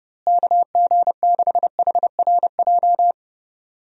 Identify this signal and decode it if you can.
Morse code, heard as a keyed tone, about 30 wpm, 700 Hz. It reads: KG6HRJ
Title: USA Call Signs - 6 Characters - Repeated 3x - 30wpm